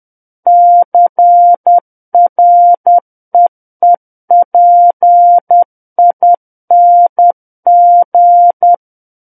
Morse code, heard as a keyed tone, about 10 wpm, 700 Hz. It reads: CREEPING
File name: Book One / 01-Chapter 1 - 10 WPM